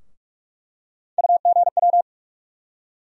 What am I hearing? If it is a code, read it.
Morse code, heard as a keyed tone, about 47 wpm, 700 Hz. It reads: UGW
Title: USA Call Sign Postfixes - 3 Letters 45wpm